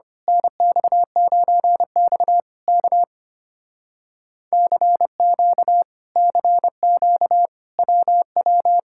Morse code, heard as a keyed tone, about 24 wpm, 700 Hz. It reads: NX9X K CQ CQ WW
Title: Speed-Racing Common QSO Phrases Encore - 25wpm